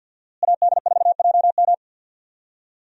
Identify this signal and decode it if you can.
Morse code, heard as a keyed tone, about 53 wpm, 700 Hz. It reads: AD4JM
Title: USA Call Signs - 5 Characters - Repeated 3x - 50wpm